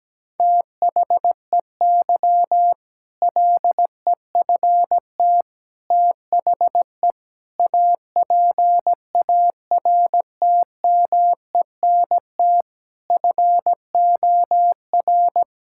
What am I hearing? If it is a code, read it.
Morse code, heard as a keyed tone, about 17 wpm, 700 Hz. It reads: THEY LEFT THE APARTMENT FOR
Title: Sentences from Top 1000 Words - Review Part 6 - 17wpm